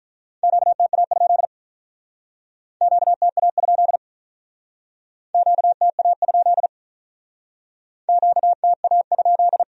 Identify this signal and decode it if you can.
Morse code, heard as a keyed tone, about 44 wpm, 700 Hz. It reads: QTA? QTA? QTA? QTA?
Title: Speed-Racing Q-Codes - 35wpm